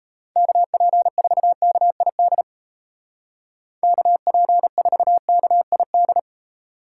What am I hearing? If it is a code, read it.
Morse code, heard as a keyed tone, about 35 wpm, 700 Hz. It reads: KP4KID KP4KID